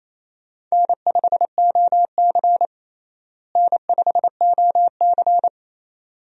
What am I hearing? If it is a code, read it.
Morse code, heard as a keyed tone, about 27 wpm, 700 Hz. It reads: N5OC N5OC
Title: USA Call Signs - 4 Characters - Repeated 2x - 28wpm